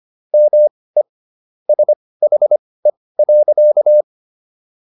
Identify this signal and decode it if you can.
Morse code, heard as a keyed tone, about 25 wpm, spaced as for 18 wpm, 600 Hz. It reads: ME SHE.